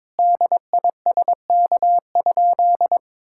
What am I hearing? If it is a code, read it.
Morse code, heard as a keyed tone, about 22 wpm, 700 Hz. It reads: DISK?
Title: Sentences from Top 1000 Words - Review Part 6 - 22wpm